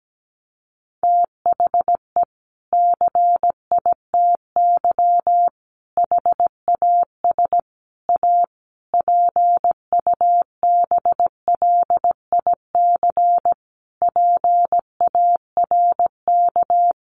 Morse code, heard as a keyed tone, about 17 wpm, 700 Hz. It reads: THE CITY HAS A PUBLIC PARK
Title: Sentences from Top 1300 Words 17wpm